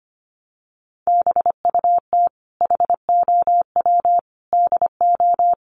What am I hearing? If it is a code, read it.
Morse code, heard as a keyed tone, about 25 wpm, 700 Hz. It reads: BUT HOW DO
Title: Sentences from Top 700 Words 25wpm